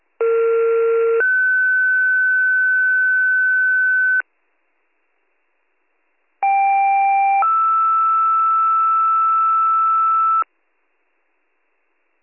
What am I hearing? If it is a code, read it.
Two-tone page, heard as a one-second tone, then a three-second tone, about 460 then 1500 Hz, about 790 then 1320 Hz. It is heard from 0.2 to 10.4 s.